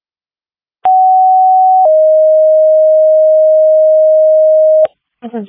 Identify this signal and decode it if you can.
Two-tone page, heard as a one-second tone, then a three-second tone, about 750 then 620 Hz. It is heard from 0.8 to 4.8 s.